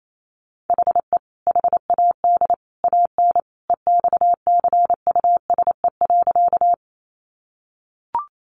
Morse code, heard as a keyed tone, about 28 wpm, 700 Hz. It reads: HE HAD AN EXCUSE.
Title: Sentences from Top 800 Words 28wpm